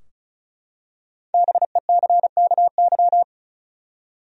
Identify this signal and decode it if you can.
Morse code, heard as a keyed tone, about 36 wpm, 700 Hz. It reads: BECKY
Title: USA Common Names - 35wpm